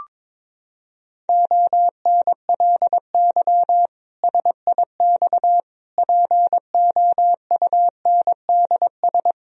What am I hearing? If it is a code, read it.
Morse code, heard as a keyed tone, about 22 wpm, 700 Hz. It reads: ONLY SIX POUNDS
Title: Sentences from Top 1000 Words - Review Part 2 - 22wpm